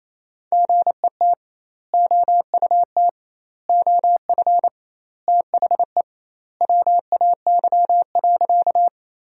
Morse code, heard as a keyed tone, about 28 wpm, 700 Hz. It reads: GET OUT OF THE WAY.
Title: 2x Word-Spacing - Sentences from Top 200 Words - 28wpm